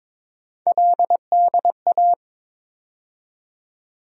Morse code, heard as a keyed tone, about 22 wpm, 700 Hz. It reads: LDA
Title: USA Call Sign Postfixes - 3 Letters 22wpm